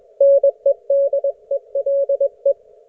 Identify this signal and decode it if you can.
Morse code, heard as a keyed tone, about 21 wpm, 560 Hz. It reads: NEDELE